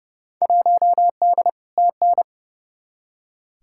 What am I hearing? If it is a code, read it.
Morse code, heard as a keyed tone, about 30 wpm, 700 Hz. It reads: 1D TN
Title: Field Day Exchanges - 30wpm